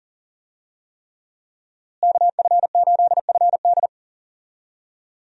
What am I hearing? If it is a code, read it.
Morse code, heard as a keyed tone, about 40 wpm, 700 Hz. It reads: KF8FD
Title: USA Call Signs - 5 Characters - Repeated 3x - 40wpm